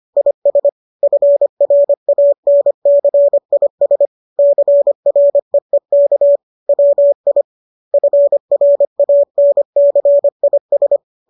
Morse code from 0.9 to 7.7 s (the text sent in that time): FRANCIS CREEK WI